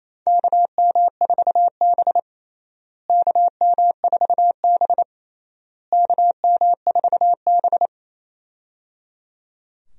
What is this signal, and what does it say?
Morse code, heard as a keyed tone, about 28 wpm, 700 Hz. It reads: KM4B KM4B KM4B
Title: USA Call Signs - 4 Characters - Repeated 3x - 28wpm